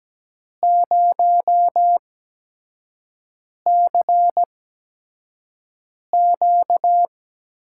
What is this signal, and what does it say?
Morse code, heard as a keyed tone, about 17 wpm, spaced as for 5 wpm, 700 Hz. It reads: 0CQ
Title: USA Call Signs - 6 Characters - 8x Character Spacing - 17wpm